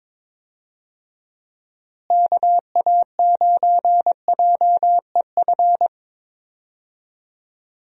Morse code, heard as a keyed tone, about 22 wpm, 700 Hz. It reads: KA9JEF
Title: USA Call Signs - 6 Characters - Repeated 2x - 22wpm